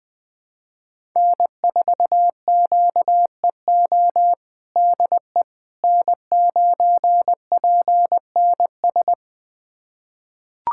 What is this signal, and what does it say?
Morse code, heard as a keyed tone, about 21 wpm, 700 Hz. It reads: N4QEO DE N9PNS
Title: USA Callsign DE Callsign - 5 Characters - 20wpm